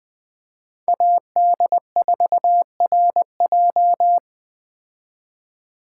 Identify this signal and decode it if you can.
Morse code, heard as a keyed tone, about 20 wpm, 700 Hz. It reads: AD4RJ